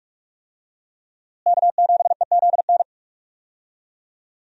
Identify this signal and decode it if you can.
Morse code, heard as a keyed tone, about 46 wpm, 700 Hz. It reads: K7EZN
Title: USA Call Signs - 5 Characters - Repeated 2x - 45wpm